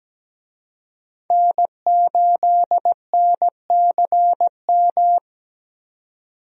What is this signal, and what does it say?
Morse code, heard as a keyed tone, about 17 wpm, 700 Hz. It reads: N8NCM
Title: USA Call Signs - 5 Characters - 17wpm